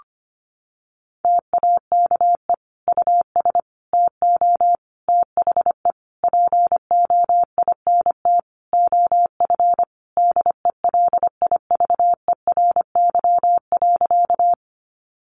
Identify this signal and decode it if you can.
Morse code, heard as a keyed tone, about 25 wpm, 700 Hz. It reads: TAKE US TO THE POINT OF DELIVERY.
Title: Sentences from Top 1500 Words 25wpm